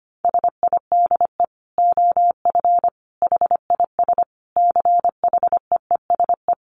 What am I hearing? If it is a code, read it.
Morse code, heard as a keyed tone, about 25 wpm, 700 Hz. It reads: SIDE OF HIS CHEESE
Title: Dit Challenge 25wpm